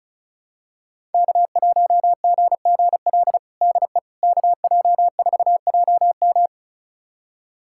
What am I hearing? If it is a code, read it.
Morse code, heard as a keyed tone, about 36 wpm, 700 Hz. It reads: K1GGL DE KJ4JM
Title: USA Callsign DE Callsign - 5 Characters - 35wpm